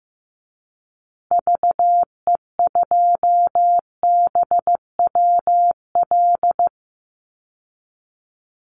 Morse code, heard as a keyed tone, about 15 wpm, 700 Hz. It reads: VE2BWL
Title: Call Signs Repeated 2x - 15wpm